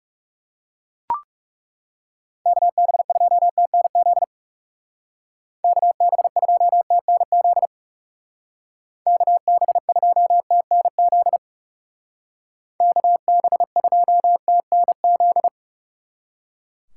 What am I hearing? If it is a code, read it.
Morse code, heard as a keyed tone, about 37 wpm, 700 Hz. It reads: KB2TNZ KB2TNZ KB2TNZ KB2TNZ
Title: Speed Racing - USA 2x3 Call Signs 30wpm